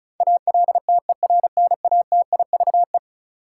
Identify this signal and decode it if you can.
Morse code, heard as a keyed tone, about 34 wpm, 700 Hz. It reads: ALTERNATIVE
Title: Sentences from Top 1000 Words - Review Part 4 - New Word Drill - 35wpm